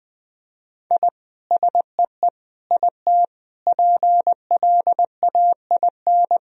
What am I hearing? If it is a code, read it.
Morse code, heard as a keyed tone, about 20 wpm, 700 Hz. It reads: I SEE IT PLAIN
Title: Sentences from Top 1000 Words - Review Part 2 - 20wpm